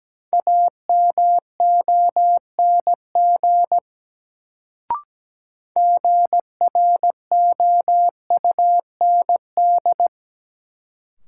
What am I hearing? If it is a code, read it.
Morse code, heard as a keyed tone, about 17 wpm, 700 Hz. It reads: AMONG GROUND
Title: Sentences from Top 500 Words - New Word Drill - 17wpm